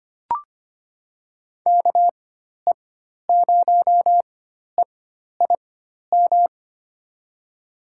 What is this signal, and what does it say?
Morse code, heard as a keyed tone, about 25 wpm, spaced as for 12 wpm, 700 Hz. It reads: KE0EIM
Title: USA Call Signs - 4x Character Spacing - 25wpm